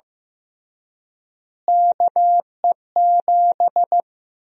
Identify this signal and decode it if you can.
Morse code, heard as a keyed tone, about 15 wpm, 700 Hz. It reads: KE7
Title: USA Call Sign Prefixes - Repeated 2x - 15wpm